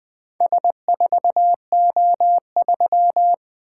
Morse code, heard as a keyed tone, about 20 wpm, 700 Hz. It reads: S4O3
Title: Four Letter-Number Combinations - 20wpm